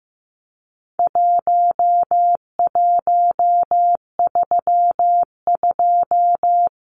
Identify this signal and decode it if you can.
Morse code, heard as a keyed tone, about 15 wpm, 700 Hz. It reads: 1132
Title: Numbers - Contest Style - 4-Digit 15wpm